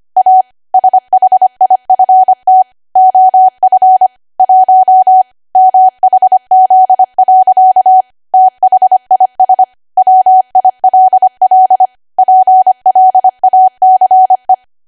Morse code, heard as a keyed tone, about 25 wpm, 750 Hz. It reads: A SHIFT OF 1 MHZ. THIS WILL PLACE
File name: Archive / 25 WPM / 25WPM